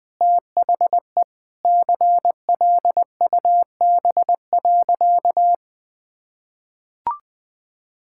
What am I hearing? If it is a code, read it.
Morse code, heard as a keyed tone, about 20 wpm, 700 Hz. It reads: THE CLUB.
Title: Sentences with 4-Word Phrases - Spoken Beginning - 20wpm